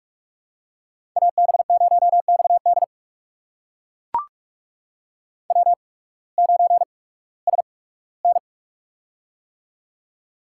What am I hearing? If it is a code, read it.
Morse code, heard as a keyed tone, about 43 wpm, spaced as for 21 wpm, 700 Hz. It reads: AB0XD W9SN
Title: USA Call Signs - 8x Character Spacing - 45wpm